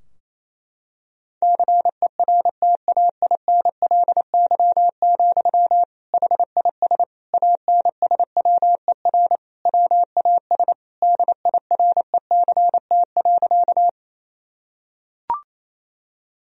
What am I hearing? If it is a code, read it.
Morse code, heard as a keyed tone, about 28 wpm, 700 Hz. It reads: CERTAINLY, HIS ANSWER WAS DIRECT.
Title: Sentences from Top 1000 Words - Review Part 2 - 28wpm